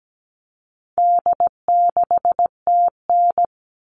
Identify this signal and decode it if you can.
Morse code, heard as a keyed tone, about 17 wpm, 700 Hz. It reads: D6TN
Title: All Cut Numbers - 4-Digit 17wpm